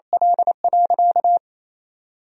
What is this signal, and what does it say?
Morse code, heard as a keyed tone, about 28 wpm, 700 Hz. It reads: L.
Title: Sentences from Top 1000 Words - Review Part 1 - 28wpm